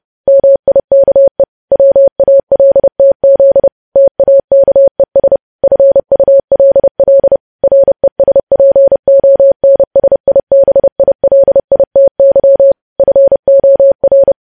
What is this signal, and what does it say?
Morse code, heard as a keyed tone, about 30 wpm, 550 Hz. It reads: MIKE WALTZ TAKES FULL RESPONSIBILITY FOR
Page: news headlines in morse code at 30 wpm